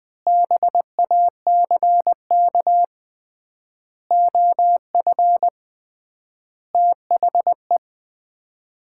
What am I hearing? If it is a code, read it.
Morse code, heard as a keyed tone, about 21 wpm, 700 Hz. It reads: BACK OF THE
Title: 3x Word-Spacing - Sets of 5 Words - 20wpm